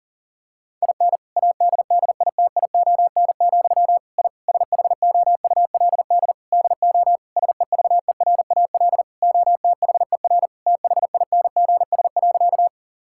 Morse code, heard as a keyed tone, about 40 wpm, 700 Hz. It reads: IN ADDITION, I SHOULD DO SEVERAL OTHER THINGS.
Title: Sentences from Top 1000 Words - Review Part 6 - 40wpm